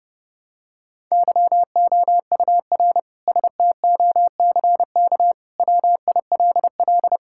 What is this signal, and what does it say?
Morse code, heard as a keyed tone, about 32 wpm, 700 Hz. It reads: YOUR STOCK WILL
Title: Sentences from Top 1000 Words - Review Part 5 - 30wpm